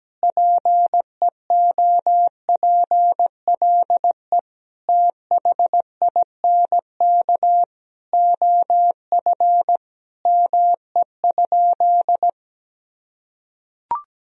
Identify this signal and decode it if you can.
Morse code, heard as a keyed tone, about 17 wpm, 700 Hz. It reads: PEOPLE THINK OF ME?
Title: Sentences from Top 1000 Words - Review Part 3 - 17wpm